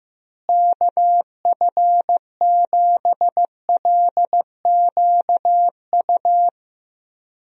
Morse code, heard as a keyed tone, about 15 wpm, 700 Hz. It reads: KF7LQU
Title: USA Call Signs - 6 Characters - Repeated 2x - 15wpm